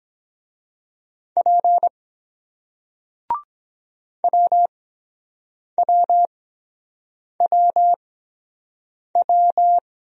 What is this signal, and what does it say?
Morse code, heard as a keyed tone, about 23 wpm, spaced as for 7 wpm, 700 Hz. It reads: P WWWW